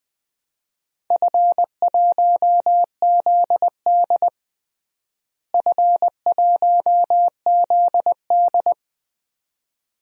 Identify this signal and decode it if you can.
Morse code, heard as a keyed tone, about 20 wpm, 700 Hz. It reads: F1ZD F1ZD
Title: DXCC-100 Call Signs - 4 Characters - Repeated 2x - 20wpm